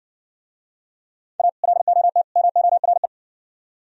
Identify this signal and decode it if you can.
Morse code, heard as a keyed tone, about 60 wpm, 700 Hz. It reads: A LOT MORE